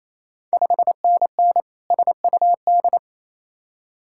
Morse code, heard as a keyed tone, about 28 wpm, 700 Hz. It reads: 5NN SUD
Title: Ontario QSO Party Exchange 28wpm